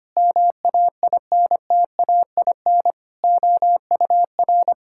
Morse code, heard as a keyed tone, about 25 wpm, 700 Hz. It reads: MAINTAIN OUR
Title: Sentences from Top 1000 Words - Review Part 5 - 25wpm